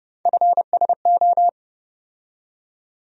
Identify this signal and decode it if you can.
Morse code, heard as a keyed tone, about 30 wpm, 700 Hz. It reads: FSO
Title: USA Call Sign Postfixes - 3 Letters 30wpm